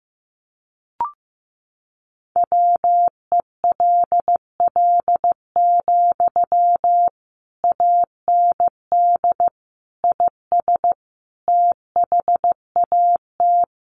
Morse code, heard as a keyed tone, about 15 wpm, 700 Hz. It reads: WELL, AND IS THAT
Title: Sentences from Top 200 Words 15wpm